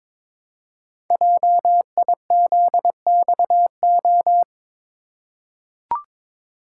Morse code, heard as a keyed tone, about 22 wpm, 700 Hz. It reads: JIZXO
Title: Code Groups - 5 Letters - 22wpm